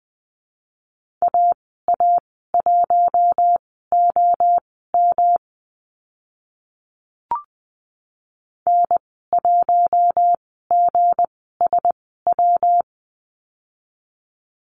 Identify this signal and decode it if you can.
Morse code, heard as a keyed tone, about 20 wpm, spaced as for 15 wpm, 700 Hz. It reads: AA1OM N1GSW
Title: 167 Lesson 27g - Number 7 - Call Signs 20wpm